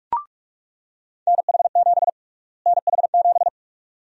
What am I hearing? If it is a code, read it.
Morse code, heard as a keyed tone, about 47 wpm, 700 Hz. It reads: NH7 NH7
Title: USA Call Sign Prefixes - Repeated 2x - 45wpm